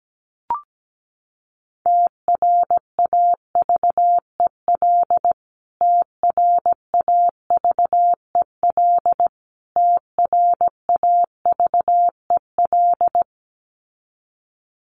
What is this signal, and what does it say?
Morse code, heard as a keyed tone, about 17 wpm, 700 Hz. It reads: TRAVEL TRAVEL TRAVEL
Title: Top 500 Words Repeated 17wpm